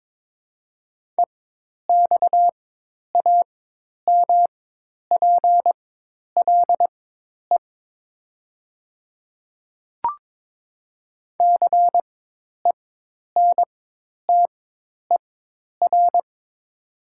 Morse code, heard as a keyed tone, about 22 wpm, spaced as for 10 wpm, 700 Hz. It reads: EXAMPLE CENTER
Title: 4x Character Spacing - Top 500 Words - 22wpm